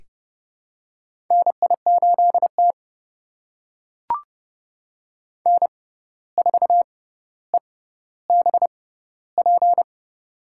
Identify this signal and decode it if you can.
Morse code, heard as a keyed tone, about 30 wpm, spaced as for 14 wpm, 700 Hz. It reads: NI8T N4EBP
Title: USA Call Signs - 6x Character Spacing - 30wpm